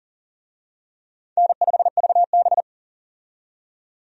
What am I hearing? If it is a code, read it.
Morse code, heard as a keyed tone, about 40 wpm, 700 Hz. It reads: N5VB
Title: USA Call Signs - 4 Characters - 40wpm